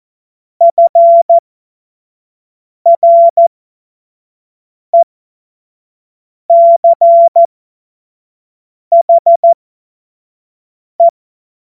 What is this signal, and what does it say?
Morse code, heard as a keyed tone, about 14 wpm, spaced as for 5 wpm, 680 Hz. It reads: FRECHE
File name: Witz-15-Juli-2025-25-F70